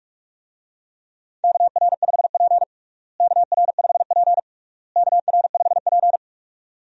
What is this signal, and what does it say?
Morse code, heard as a keyed tone, about 46 wpm, 700 Hz. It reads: KR5P KR5P KR5P
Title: USA Call Signs - 4 Characters - Repeated 3x - 45wpm